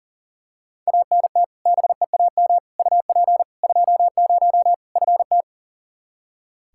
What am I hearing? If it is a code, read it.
Morse code, heard as a keyed tone, about 40 wpm, 700 Hz. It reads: ANT BEAM UP 20 FT
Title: K1USN SST QSO Practice 40wpm